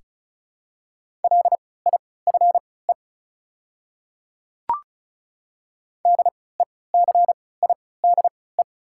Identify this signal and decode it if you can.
Morse code, heard as a keyed tone, about 36 wpm, spaced as for 20 wpm, 690 Hz. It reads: LIFE DECIDE
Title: 3x Character Spacing - Top 500 Words - 35wpm